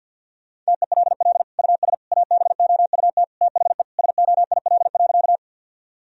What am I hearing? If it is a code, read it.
Morse code, heard as a keyed tone, about 50 wpm, 700 Hz. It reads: TELL US ABOUT THE SOIL.
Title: Sentences from Top 1000 Words - Review Part 4 - 50wpm